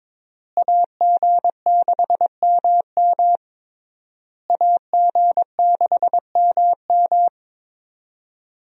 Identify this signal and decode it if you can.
Morse code, heard as a keyed tone, about 21 wpm, 700 Hz. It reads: AG6MM AG6MM
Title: USA Call Signs - 5 Characters - Repeated 3x - 22wpm